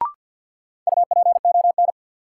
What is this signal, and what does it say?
Morse code, heard as a keyed tone, about 51 wpm, 700 Hz. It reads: UPON
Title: Sentences from Top 1000 Words - Review Part 6 - New Word Drill - 50wpm